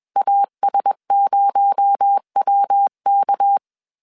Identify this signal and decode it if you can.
Morse code, heard as a keyed tone, about 21 wpm, 800 Hz. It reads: AS0WK
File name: CW / DET / 1 6 1500C S C 20wpm 2116